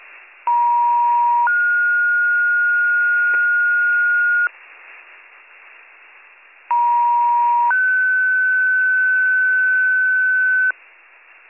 Two-tone page, heard as a one-second tone, then a three-second tone, about 950 then 1430 Hz, about 950 then 1500 Hz, with noise. It is heard from 0.5 to 10.7 s.